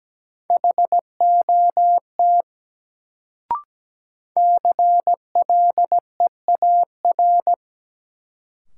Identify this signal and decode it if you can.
Morse code, heard as a keyed tone, about 17 wpm, 700 Hz. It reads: HOT CLEAR
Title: Sentences from Top 500 Words - New Word Drill - 17wpm